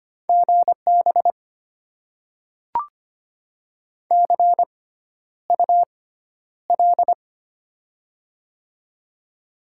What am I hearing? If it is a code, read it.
Morse code, heard as a keyed tone, about 25 wpm, spaced as for 11 wpm, 700 Hz. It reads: GB CUL
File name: Top 90 QSO Elements - 6x Character Spacing - 25wpm